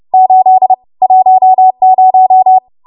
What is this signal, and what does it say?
Morse code, heard as a keyed tone, about 30 wpm, 750 Hz. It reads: 8 10